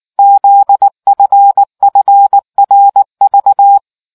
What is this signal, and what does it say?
Morse code, heard as a keyed tone, about 19 wpm, 800 Hz. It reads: ZFFRV